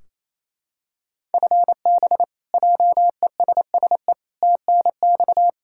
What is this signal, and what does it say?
Morse code, heard as a keyed tone, about 28 wpm, 700 Hz. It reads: FB JESSE TNX
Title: Practice QSO - Exchange 3 - More Info - 28wpm